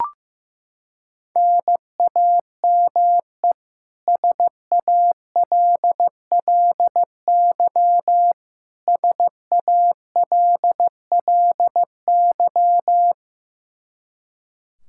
Morse code, heard as a keyed tone, about 15 wpm, 700 Hz. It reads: NAME SALLY SALLY
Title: K1USN SST QSO Practice 15wpm